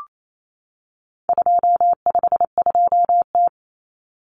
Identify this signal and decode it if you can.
Morse code, heard as a keyed tone, about 28 wpm, 700 Hz. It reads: 252T
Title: Common Cut Numbers - 4-Digit 28wpm